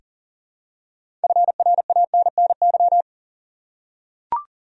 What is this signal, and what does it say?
Morse code, heard as a keyed tone, about 40 wpm, 700 Hz. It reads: FRANNY
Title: USA Common Names - 40wpm